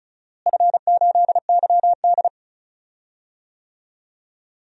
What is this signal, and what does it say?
Morse code, heard as a keyed tone, about 36 wpm, 700 Hz. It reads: F8YD
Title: DXCC-100 Call Signs - 4 Characters - Repeated 2x - 35wpm